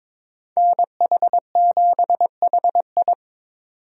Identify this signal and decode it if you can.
Morse code, heard as a keyed tone, about 26 wpm, 700 Hz. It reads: NH7HI
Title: Speed Racing - USA Call Signs 22wpm